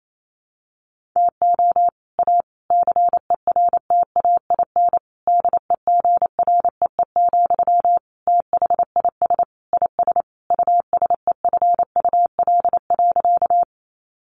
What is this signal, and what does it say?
Morse code, heard as a keyed tone, about 28 wpm, 700 Hz. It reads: TO A CERTAIN DEGREE, THIS IS USEFUL.
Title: Sentences from Top 1500 Words 28wpm